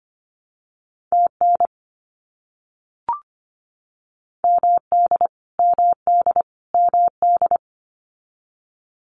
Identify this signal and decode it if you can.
Morse code, heard as a keyed tone, about 25 wpm, 700 Hz. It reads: TN MD MD MD
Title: US States Repeated 3x 25wpm